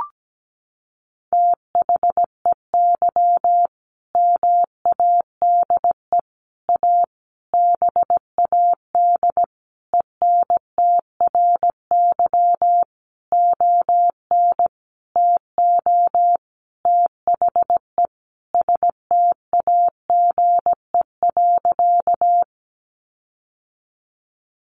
Morse code, heard as a keyed tone, about 17 wpm, 700 Hz. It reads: THEY MADE A BAD ENTRY ON TO THE STAGE.
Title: Sentences from Top 1500 Words 17wpm